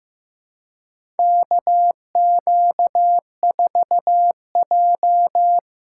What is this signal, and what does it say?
Morse code, heard as a keyed tone, about 15 wpm, 700 Hz. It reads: KQ4J
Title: USA Callsign DE Callsign - 4 Characters - 15wpm